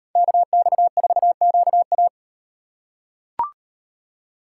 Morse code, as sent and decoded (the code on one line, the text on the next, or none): -.- -..- ....- --.- .-
KX4QA